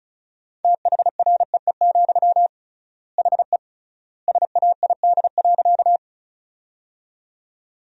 Morse code, heard as a keyed tone, about 36 wpm, 700 Hz. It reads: THREE, HE SAID.
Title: 3x Word-Spacing - Sentences from Top 200 Words - 35wpm